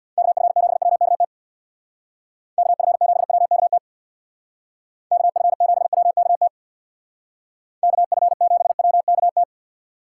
Morse code, heard as a keyed tone, about 59 wpm, 700 Hz. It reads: KF7WKT KF7WKT KF7WKT KF7WKT
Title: Speed Racing - USA Call Signs 50wpm